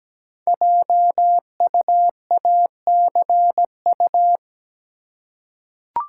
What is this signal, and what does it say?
Morse code, heard as a keyed tone, about 17 wpm, 700 Hz. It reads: JUACU
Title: Code Groups - 5 Letters - 17wpm